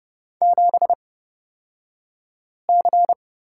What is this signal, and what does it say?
Morse code, heard as a keyed tone, about 30 wpm, spaced as for 5 wpm, 700 Hz. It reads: 7C
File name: Lesson 37d - Letter x - Two Characters 30wpm